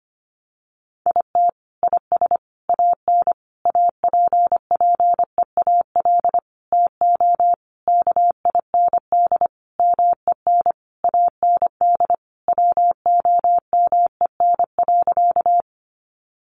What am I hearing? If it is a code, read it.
Morse code, heard as a keyed tone, about 25 wpm, 700 Hz. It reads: IT IS AN APPEAL TO KIND MEN AND WOMEN.
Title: Sentences from Top 1300 Words 25wpm